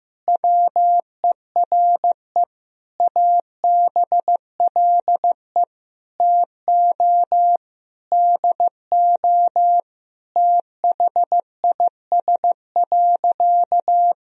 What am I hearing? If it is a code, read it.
Morse code, heard as a keyed tone, about 15 wpm, 700 Hz. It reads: WERE ABLE TO DO THIS.
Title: Sentences from Top 1000 Words - Review Part 2 - 15wpm